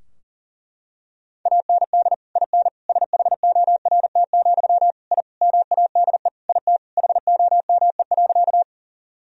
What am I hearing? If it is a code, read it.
Morse code, heard as a keyed tone, about 40 wpm, 700 Hz. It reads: AND IN SHORT, I MADE IT HOME.
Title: Sentences from Top 1000 Words - Review Part 2 - 40wpm